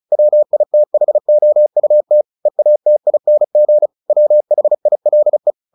Morse code, heard as a keyed tone, about 34 wpm, 600 Hz. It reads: WITHOUT EATING WHILE
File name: the old man and the sea / the old man and the sea 1 35wpm